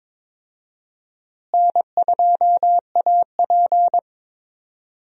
Morse code, heard as a keyed tone, about 22 wpm, 700 Hz. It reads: N2AP
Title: USA Call Signs - 4 Characters - Repeated 3x - 22wpm